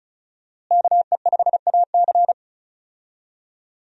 Morse code, heard as a keyed tone, about 34 wpm, 700 Hz. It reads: KE5AC
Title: USA Call Signs - 5 Characters - Repeated 2x - 35wpm